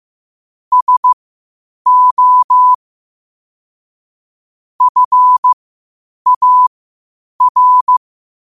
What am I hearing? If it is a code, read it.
Morse code, heard as a keyed tone, about 15 wpm, spaced as for 8 wpm, 1000 Hz. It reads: SO FAR
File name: qotdfiles / 8wpm